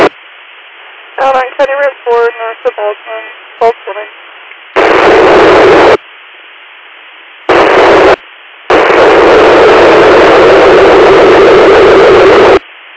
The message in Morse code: none

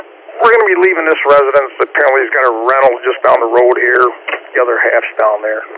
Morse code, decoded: none

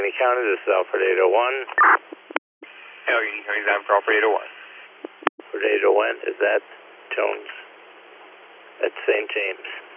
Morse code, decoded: none

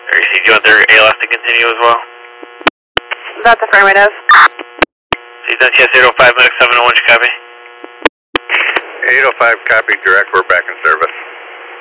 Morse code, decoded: none

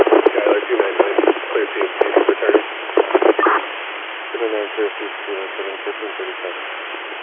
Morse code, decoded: none

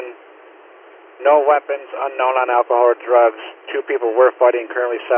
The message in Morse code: none

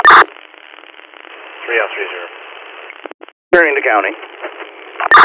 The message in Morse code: none

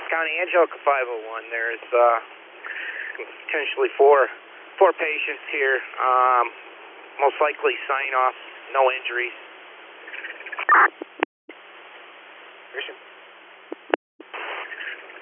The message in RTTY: none

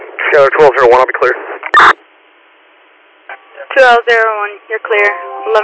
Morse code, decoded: none

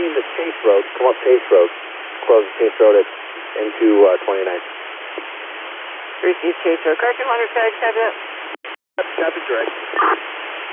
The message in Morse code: none

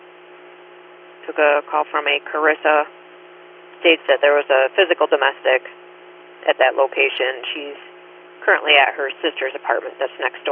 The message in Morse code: none

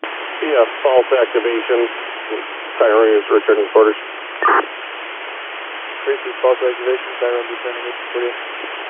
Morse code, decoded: none

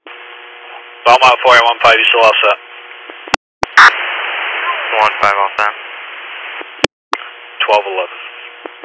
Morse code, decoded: none

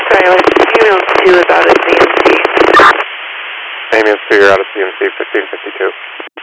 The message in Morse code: none

